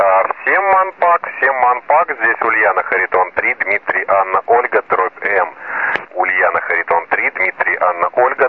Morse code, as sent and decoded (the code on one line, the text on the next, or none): none